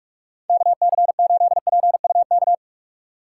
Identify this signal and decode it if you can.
Morse code, heard as a keyed tone, about 43 wpm, 700 Hz. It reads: KC8PUK
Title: USA Call Signs - 6x Character Spacing - 45wpm